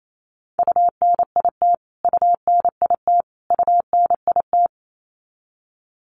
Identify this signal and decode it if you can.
Morse code, heard as a keyed tone, about 28 wpm, 700 Hz. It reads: UNIT UNIT UNIT
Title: Top 500 Words Repeated 28wpm